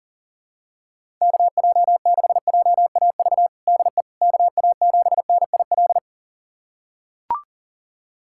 Morse code, heard as a keyed tone, about 40 wpm, 700 Hz. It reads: KJ6JAV DE KA7NIL
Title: USA Callsign DE Callsign 40wpm